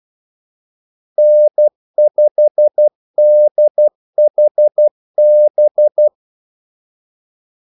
Morse code, heard as a keyed tone, about 12 wpm, 600 Hz. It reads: N5DHB
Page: 2 ekstra ordmellemrum mellem kaldesignalerne